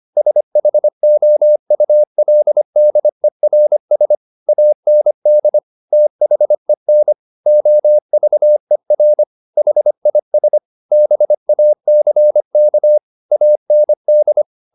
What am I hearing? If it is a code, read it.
Morse code, heard as a keyed tone, about 25 wpm, 600 Hz. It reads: SHOULDERS AND THEN OVER HIS BACK AND